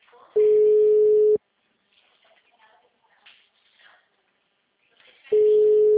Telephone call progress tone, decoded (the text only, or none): ringback tone